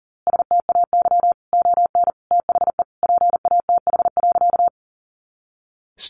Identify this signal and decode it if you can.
Morse code, heard as a keyed tone, about 40 wpm, 700 Hz. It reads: STAY ON THE PATH.
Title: A Sound Of Thunder 40wpm